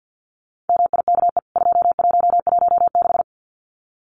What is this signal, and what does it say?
Morse code, heard as a keyed tone, about 48 wpm, 700 Hz. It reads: MIKE 2116
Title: CWT Exchange 50wpm